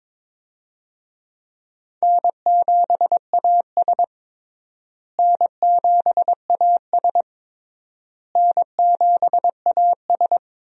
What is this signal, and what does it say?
Morse code, heard as a keyed tone, about 22 wpm, 700 Hz. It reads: N7AS N7AS N7AS
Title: USA Call Signs - 4 Characters - Repeated 3x - 22wpm